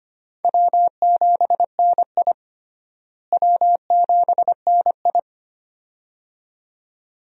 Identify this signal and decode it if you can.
Morse code, heard as a keyed tone, about 25 wpm, 700 Hz. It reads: W7NI W7NI
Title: USA Call Signs - 4 Characters - Repeated 2x - 25wpm